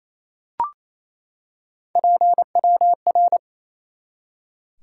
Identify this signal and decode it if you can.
Morse code, heard as a keyed tone, about 28 wpm, 700 Hz. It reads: PWR
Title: Top 90 QSO Elements 28wpm